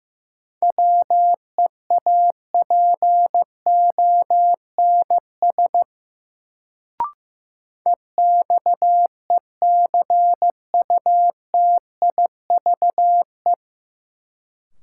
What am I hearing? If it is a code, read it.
Morse code, heard as a keyed tone, about 16 wpm, 700 Hz. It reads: WEAPONS EXECUTIVE
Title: Sets of 5 Words Encore - New Word Drill - 15wpm